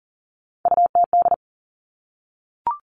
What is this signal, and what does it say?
Morse code, heard as a keyed tone, about 40 wpm, 700 Hz. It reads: UTD
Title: Three Challenging Letter-Number Combinations 40wpm